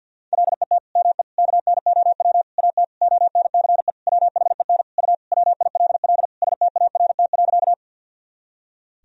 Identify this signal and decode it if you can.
Morse code, heard as a keyed tone, about 50 wpm, 700 Hz. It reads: LET ME KNOW AT ONCE WHEN U WILL START.
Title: Sentences from Top 300 Words and Abbreviations 50wpm